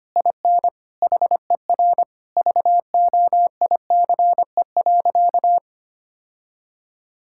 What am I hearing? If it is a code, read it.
Morse code, heard as a keyed tone, about 25 wpm, 700 Hz. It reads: IN HER VOICE.
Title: Sentences from Top 1600 Words 25wpm